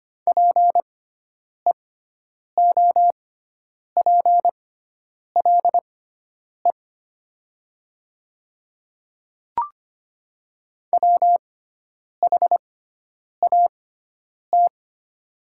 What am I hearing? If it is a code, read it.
Morse code, heard as a keyed tone, about 25 wpm, spaced as for 9 wpm, 700 Hz. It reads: PEOPLE WHAT
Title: 6x Character Spacing - Top 100 Words - 25wpm